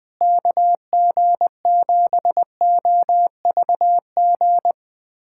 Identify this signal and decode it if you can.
Morse code, heard as a keyed tone, about 20 wpm, 700 Hz. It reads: KG7OVG
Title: USA Callsign DE Callsign - 6 Characters - 20wpm